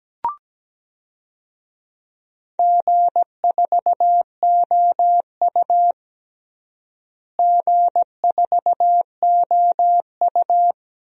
Morse code, heard as a keyed tone, about 17 wpm, 700 Hz. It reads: G4OU G4OU
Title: DXCC-100 Call Signs - 4 Characters - Repeated 2x - 17wpm